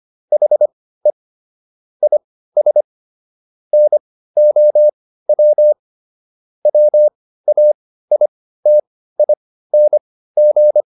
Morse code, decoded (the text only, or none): HE IS NOW WAITING